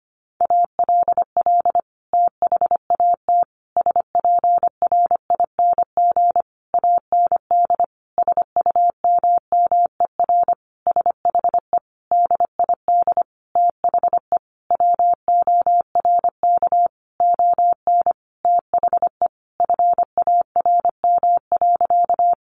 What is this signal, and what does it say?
Morse code, heard as a keyed tone, about 25 wpm, 700 Hz. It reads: ALL THAT SPRING AND SUMMER SHE DID THE WORK ON THE FARM.
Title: Sentences from Top 1200 Words 25wpm